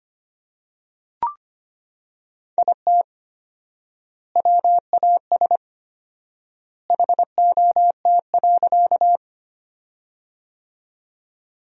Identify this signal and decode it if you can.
Morse code, heard as a keyed tone, about 25 wpm, 700 Hz. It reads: IT WAS HOT.
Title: 4x Word-Spacing - Sentences from Top 500 Words - 25wpm